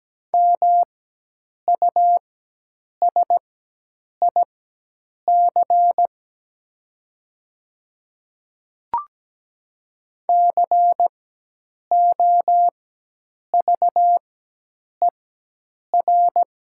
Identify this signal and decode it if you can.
Morse code, heard as a keyed tone, about 17 wpm, spaced as for 8 wpm, 700 Hz. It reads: MUSIC COVER
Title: 4x Character Spacing - Top 300 Words - 17wpm